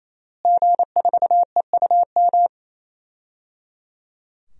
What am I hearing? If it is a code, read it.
Morse code, heard as a keyed tone, about 28 wpm, 700 Hz. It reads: G4EUM
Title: DXCC-100 Call Signs - 5 Characters - Repeated 3x - 28wpm